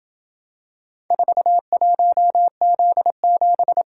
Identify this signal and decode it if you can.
Morse code, heard as a keyed tone, about 26 wpm, 700 Hz. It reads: 41Z7